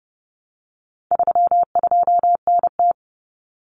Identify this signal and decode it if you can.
Morse code, heard as a keyed tone, about 30 wpm, 700 Hz. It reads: 32NT
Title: Common Cut Numbers - 4-Digit 30wpm